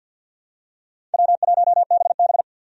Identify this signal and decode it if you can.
Morse code, heard as a keyed tone, about 47 wpm, 700 Hz. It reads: W1BB
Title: USA Call Signs - 4 Characters - 50wpm